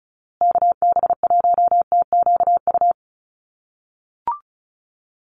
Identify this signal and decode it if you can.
Morse code, heard as a keyed tone, about 34 wpm, 700 Hz. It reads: KB1TQU
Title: USA Call Signs Repeated 2x 35wpm